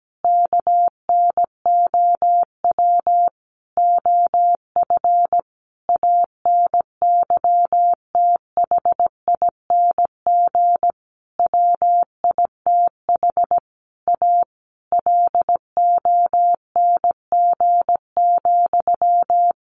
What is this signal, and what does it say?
Morse code, heard as a keyed tone, about 17 wpm, 700 Hz. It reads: KNOW OF ANYTHING WITH A LONG,
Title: Sentences from Top 1200 Words 17wpm